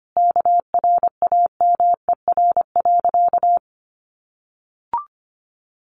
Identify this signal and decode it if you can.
Morse code, heard as a keyed tone, about 25 wpm, 700 Hz. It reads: KRAMER.
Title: A Sound Of Thunder 25wpm